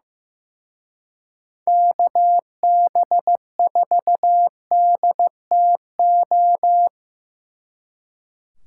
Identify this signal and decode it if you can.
Morse code, heard as a keyed tone, about 16 wpm, 700 Hz. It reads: KB4DTO
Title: Speed Racing - USA 2x3 Call Signs 15wpm